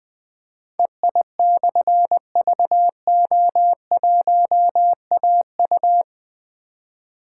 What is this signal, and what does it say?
Morse code, heard as a keyed tone, about 20 wpm, 700 Hz. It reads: EI/VO1AU
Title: Difficult Call Signs 20wpm